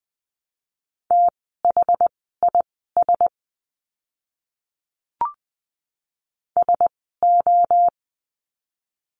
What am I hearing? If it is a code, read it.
Morse code, heard as a keyed tone, about 20 wpm, spaced as for 14 wpm, 700 Hz. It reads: THIS SO
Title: 230 Lesson 37f - Letter x - Words 20wpm